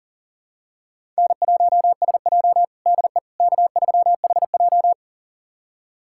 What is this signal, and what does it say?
Morse code, heard as a keyed tone, about 40 wpm, 700 Hz. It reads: N1SJ DE K3HJ
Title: USA Callsign DE Callsign - 4 Characters - 40wpm